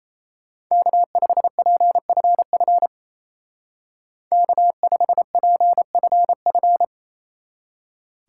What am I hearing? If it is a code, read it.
Morse code, heard as a keyed tone, about 32 wpm, 700 Hz. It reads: K5PFF K5PFF
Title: Speed Racing - USA Call Signs 28wpm